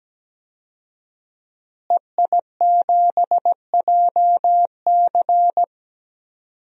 Morse code, heard as a keyed tone, about 17 wpm, 700 Hz. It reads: EI7JC
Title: DXCC-100 Call Signs - 5 Characters - Repeated 2x - 17wpm